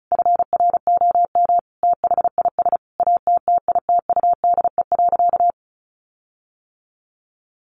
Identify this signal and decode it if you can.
Morse code, heard as a keyed tone, about 34 wpm, 700 Hz. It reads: FROM THIS ATTITUDE.
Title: Sentences from Top 1300 Words 35wpm